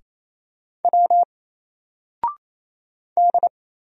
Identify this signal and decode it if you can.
Morse code, heard as a keyed tone, about 27 wpm, 700 Hz. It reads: W D
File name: Single-Letter-Number-28wpm